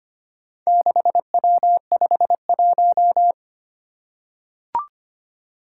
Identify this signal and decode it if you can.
Morse code, heard as a keyed tone, about 25 wpm, 700 Hz. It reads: 6W51
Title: Four Challenging Letter-Number Combinations 25wpm